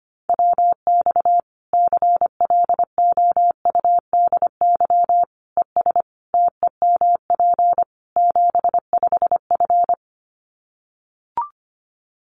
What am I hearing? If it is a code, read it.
Morse code, heard as a keyed tone, about 25 wpm, 700 Hz. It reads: WX CLOUDY ES TEMP 75F
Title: Common QSO Phrases 25wpm